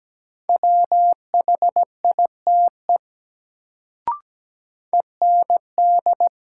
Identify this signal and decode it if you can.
Morse code, heard as a keyed tone, about 17 wpm, 700 Hz. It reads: WHITE END
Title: Sets of 2 Words - New Word Drill - 17wpm